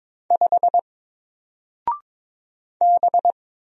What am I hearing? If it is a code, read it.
Morse code, heard as a keyed tone, about 22 wpm, 700 Hz. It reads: 5 B
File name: Single-Letter-Number-22wpm